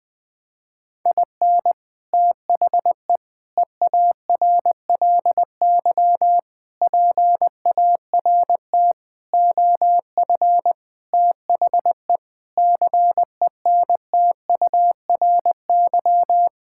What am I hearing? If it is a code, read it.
Morse code, heard as a keyed tone, about 21 wpm, 700 Hz. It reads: IN THE EARLY PART OF THE CENTURY
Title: Sentences with 6-Word Phrases - 20wpm